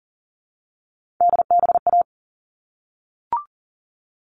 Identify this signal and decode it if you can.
Morse code, heard as a keyed tone, about 40 wpm, 700 Hz. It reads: DBA
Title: All Cut Numbers - 3-Digit 40wpm